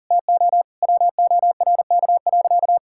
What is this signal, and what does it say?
Morse code, heard as a keyed tone, about 40 wpm, 700 Hz. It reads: TO WORK.
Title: Sentences from Top 1000 Words - Review Part 6 - 40wpm